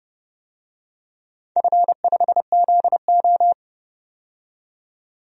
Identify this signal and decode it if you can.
Morse code, heard as a keyed tone, about 30 wpm, 700 Hz. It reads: F5ZO
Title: DXCC-100 Call Signs - 4 Characters - 30wpm